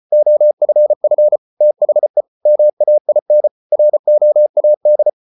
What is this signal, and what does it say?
Morse code, heard as a keyed tone, about 34 wpm, 600 Hz. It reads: OFF THE MAIN ROAD